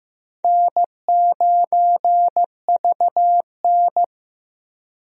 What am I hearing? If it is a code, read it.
Morse code, heard as a keyed tone, about 15 wpm, 700 Hz. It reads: N9VN
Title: USA Call Signs - 4 Characters - Repeated 3x - 15wpm